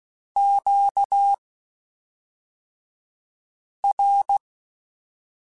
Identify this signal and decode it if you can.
Morse code, heard as a keyed tone, about 16 wpm, spaced as for 3 wpm, 800 Hz. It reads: QR
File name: (spaced) / Q-Calls Spaced-mp3s / Q Calls sp Once @5